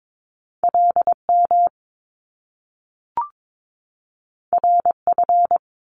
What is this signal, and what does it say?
Morse code, heard as a keyed tone, about 22 wpm, 700 Hz. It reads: LM RF
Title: Top Two Letter Combinations in Words 22wpm